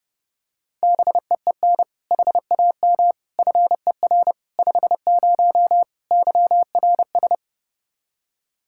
Morse code, heard as a keyed tone, about 30 wpm, 700 Hz. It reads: BEEN HAM FER 50 YRS
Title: K1USN SST QSO Practice 30wpm